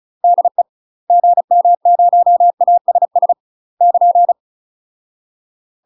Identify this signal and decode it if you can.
Morse code, heard as a keyed tone, about 36 wpm, 700 Hz. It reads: DE GM0ASS <KN>
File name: UK MASTER QSOs TRAINING / 35WPM / UK MASTER QSOs 0014